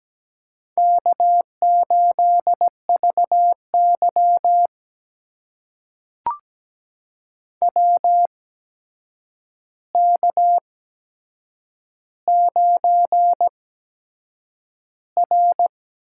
Morse code, heard as a keyed tone, about 17 wpm, spaced as for 7 wpm, 700 Hz. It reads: K8VY WK9R
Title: USA Call Signs - 4 Characters - 8x Character Spacing - 17wpm